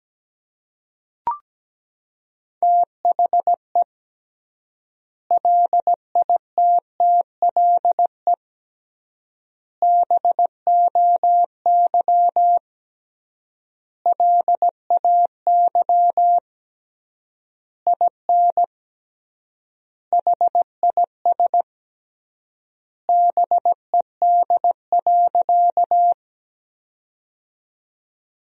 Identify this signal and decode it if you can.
Morse code, heard as a keyed tone, about 17 wpm, 700 Hz. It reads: THE LITTLE BOY LAY IN HIS BED.
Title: 3x Word-Spacing - Sentences from Top 500 Words - 17wpm